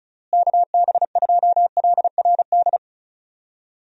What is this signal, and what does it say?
Morse code, heard as a keyed tone, about 36 wpm, 700 Hz. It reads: KB2LRD
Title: USA Call Signs - 6 Characters - Repeated 3x - 35wpm